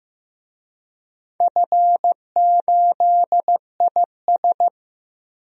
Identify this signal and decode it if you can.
Morse code, heard as a keyed tone, about 15 wpm, 700 Hz. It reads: F8IS
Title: DXCC-100 Call Signs - 4 Characters - 15wpm